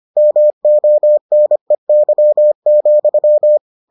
Morse code, heard as a keyed tone, about 25 wpm, 600 Hz. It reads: MONEY,